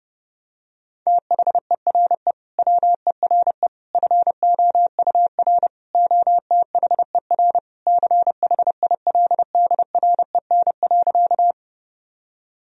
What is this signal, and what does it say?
Morse code, heard as a keyed tone, about 30 wpm, 700 Hz. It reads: THERE WERE FOUR OTHER CHILDREN.
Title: Sentences from Top 1000 Words - Review Part 1 - 30wpm